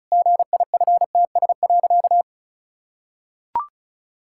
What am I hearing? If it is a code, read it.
Morse code, heard as a keyed tone, about 36 wpm, 700 Hz. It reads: GIFTS.
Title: Pangram Challenge v2 - 35wpm